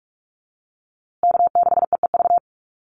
Morse code, heard as a keyed tone, about 44 wpm, 700 Hz. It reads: K6EEV
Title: USA Call Signs Repeated 2x 45wpm